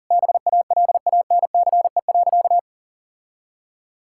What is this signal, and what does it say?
Morse code, heard as a keyed tone, about 40 wpm, 700 Hz. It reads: BALANCE.
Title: Sentences from Top 1000 Words - Review Part 6 - 40wpm